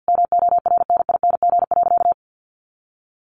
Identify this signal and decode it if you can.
Morse code, heard as a keyed tone, about 50 wpm, 700 Hz. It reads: MORNING.
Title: A Sound Of Thunder 50wpm